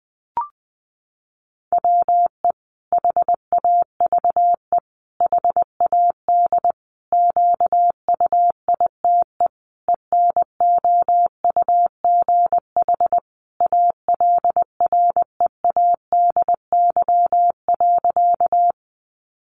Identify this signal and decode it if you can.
Morse code, heard as a keyed tone, about 19 wpm, 700 Hz. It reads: WE HAVE HAD QUITE ENOUGH ALREADY.
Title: Sentences from Top 700 Words 20wpm